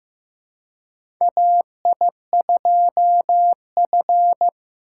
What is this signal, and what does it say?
Morse code, heard as a keyed tone, about 15 wpm, 700 Hz. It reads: AI2F
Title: USA Call Signs - 4 Characters - 2x Character Spacing - 15wpm